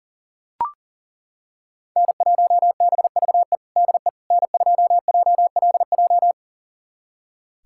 Morse code, heard as a keyed tone, about 40 wpm, 700 Hz. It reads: N1BVE DE N2JLJ
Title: USA Callsign DE Callsign - 5 Characters - 40wpm